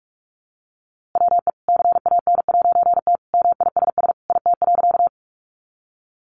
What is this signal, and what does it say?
Morse code, heard as a keyed tone, about 44 wpm, 700 Hz. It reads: WE CAN'T MISS IT.
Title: A Sound Of Thunder 45wpm